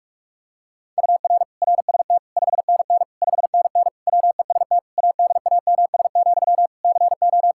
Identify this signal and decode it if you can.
Morse code, heard as a keyed tone, about 46 wpm, 700 Hz. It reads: UR RST 5NN 5NN WEST ADAMS, CO
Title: K1USN SST QSO Practice 45wpm